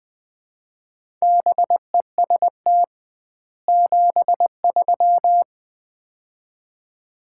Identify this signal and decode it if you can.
Morse code, heard as a keyed tone, about 21 wpm, 700 Hz. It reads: BEST 73
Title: Common QSO Phrases - 2x Word-Spacing - 20wpm